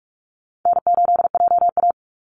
Morse code, heard as a keyed tone, about 44 wpm, 700 Hz. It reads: N8JA
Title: Call Signs Repeated 2x - 45wpm